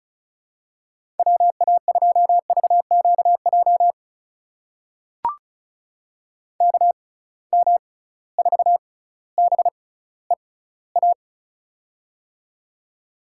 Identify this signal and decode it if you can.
Morse code, heard as a keyed tone, about 36 wpm, spaced as for 18 wpm, 700 Hz. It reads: WA2VQJ KM4BEA
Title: USA Call Signs - 6 Characters - 6x Character Spacing - 35wpm